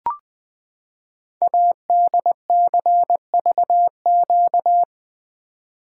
Morse code, heard as a keyed tone, about 20 wpm, 700 Hz. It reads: ADCVQ
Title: Code Groups - 5 Letters - 20wpm